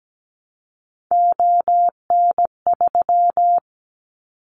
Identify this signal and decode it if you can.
Morse code, heard as a keyed tone, about 17 wpm, 700 Hz. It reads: ON3
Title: Callsign Prefixes for Top 100 DXCC - 17wpm